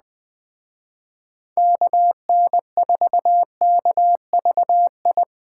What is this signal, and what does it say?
Morse code, heard as a keyed tone, about 21 wpm, 700 Hz. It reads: KN4KVI
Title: Speed Racing - USA 2x3 Call Signs 17wpm